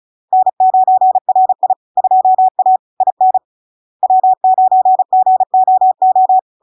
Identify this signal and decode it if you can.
Morse code, heard as a keyed tone, about 35 wpm, 750 Hz. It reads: N9RI 2A IN W9GOO
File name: LARK-FD-2022-Log-in-CW-35wpm-750-Hz